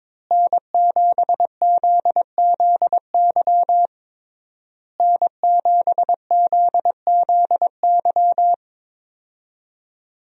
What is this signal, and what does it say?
Morse code, heard as a keyed tone, about 22 wpm, 700 Hz. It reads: N7ZZY N7ZZY
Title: USA Call Signs - 5 Characters - Repeated 3x - 22wpm